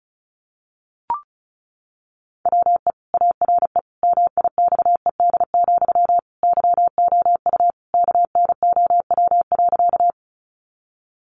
Morse code, heard as a keyed tone, about 34 wpm, 700 Hz. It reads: WE ARE MIXED, YOU KNOW.
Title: Sentences from Top 1500 Words 35wpm